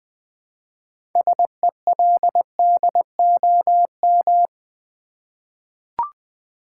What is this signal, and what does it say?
Morse code, heard as a keyed tone, about 20 wpm, 700 Hz. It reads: SELDOM
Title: 6 Letter Words - First 3 Letters Given - 20wpm